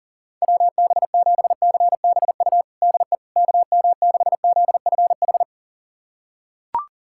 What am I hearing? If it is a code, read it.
Morse code, heard as a keyed tone, about 40 wpm, 700 Hz. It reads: WB7CBU DE KM6ZFH
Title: USA Callsign DE Callsign - 6 Characters - 40wpm